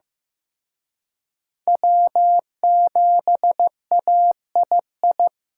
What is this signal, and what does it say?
Morse code, heard as a keyed tone, about 18 wpm, 700 Hz. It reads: W7AII
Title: Speed Racing - USA Call Signs 15wpm